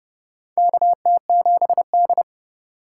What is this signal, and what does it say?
Morse code, heard as a keyed tone, about 28 wpm, 700 Hz. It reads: KT7D
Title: USA Call Signs - 4 Characters - 3x Character Spacing - 30wpm